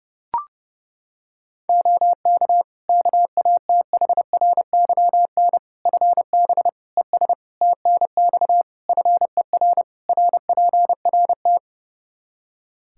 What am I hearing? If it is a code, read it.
Morse code, heard as a keyed tone, about 30 wpm, 700 Hz. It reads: OK KATHRYN FB ES TNX FER RPRT
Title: Practice QSO - Exchange 2 - Info - 30wpm